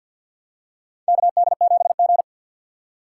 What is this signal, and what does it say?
Morse code, heard as a keyed tone, about 53 wpm, 700 Hz. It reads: KD7G
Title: USA Call Signs - 6x Character Spacing - 50wpm